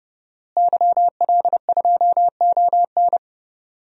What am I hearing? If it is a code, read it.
Morse code, heard as a keyed tone, about 30 wpm, 700 Hz. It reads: YL2ON
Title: DXCC-100 Call Signs - 5 Characters - Repeated 2x - 30wpm